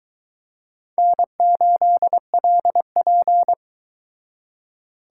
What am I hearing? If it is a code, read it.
Morse code, heard as a keyed tone, about 22 wpm, 700 Hz. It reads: N8LP